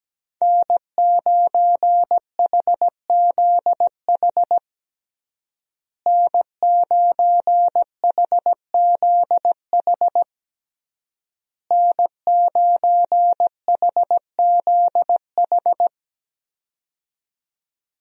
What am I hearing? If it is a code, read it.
Morse code, heard as a keyed tone, about 17 wpm, 700 Hz. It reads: N9HZH N9HZH N9HZH
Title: USA Call Signs - 5 Characters - Repeated 3x - 17wpm